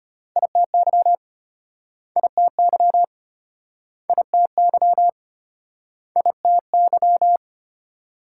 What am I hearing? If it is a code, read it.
Morse code, heard as a keyed tone, about 34 wpm, 700 Hz. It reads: ITY ITY ITY ITY